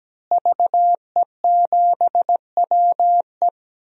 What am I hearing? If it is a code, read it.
Morse code, heard as a keyed tone, about 17 wpm, 700 Hz. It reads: VE7WE
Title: DXCC-100 Call Signs - 5 Characters - Repeated 3x - 17wpm